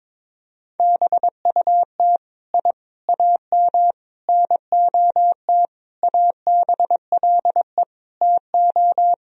Morse code, heard as a keyed tone, about 22 wpm, 700 Hz. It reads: BUT I AM NOT ABLE TO
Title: Sentences from Top 1000 Words - Review Part 2 - 22wpm